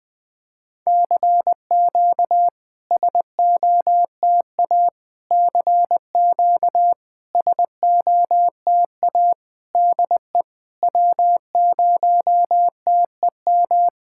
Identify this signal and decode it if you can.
Morse code, heard as a keyed tone, about 20 wpm, 700 Hz. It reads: CQ SOTA CQ SOTA DE W0TEM
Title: SOTA QSO Practice 20wpm